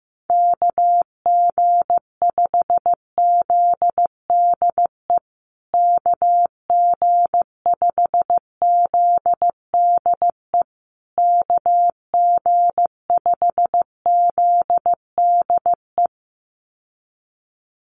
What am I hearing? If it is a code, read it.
Morse code, heard as a keyed tone, about 15 wpm, 700 Hz. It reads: KG5ZDE KG5ZDE KG5ZDE
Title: USA Call Signs Repeated 3x 15wpm